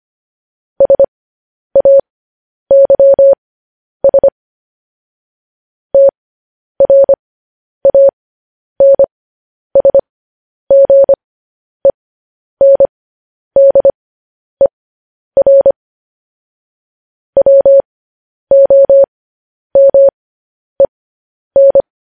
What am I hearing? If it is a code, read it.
Morse code, heard as a keyed tone, about 25 wpm, spaced as for 10 wpm, 550 Hz. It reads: SAYS TRANSGENDER WOMEN
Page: news-headlines-in-morse-code-at-10-wpm